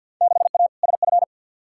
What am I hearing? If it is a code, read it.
Morse code, heard as a keyed tone, about 53 wpm, 700 Hz. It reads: 6A SF
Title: Field Day Exchanges - 50wpm